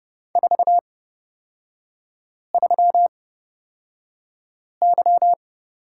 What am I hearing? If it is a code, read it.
Morse code, heard as a keyed tone, about 30 wpm, spaced as for 5 wpm, 700 Hz. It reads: 43Y
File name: Lesson 37e - Letter x - Three Characters 30wpm